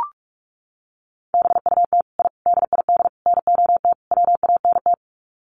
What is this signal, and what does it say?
Morse code, heard as a keyed tone, about 44 wpm, 700 Hz. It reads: BUT I DID NOT WANT
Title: Sets of 5 Words Encore 45wpm